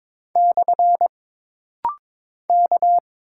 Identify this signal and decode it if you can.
Morse code, heard as a keyed tone, about 22 wpm, 700 Hz. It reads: / K
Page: This rapid-fire timing is geared to push your ICR (Instant Character Recognition) skills.